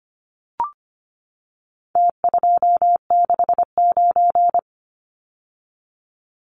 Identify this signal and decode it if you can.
Morse code, heard as a keyed tone, about 25 wpm, 700 Hz. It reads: T269
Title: Numbers - Contest Style - 4-Digit 25wpm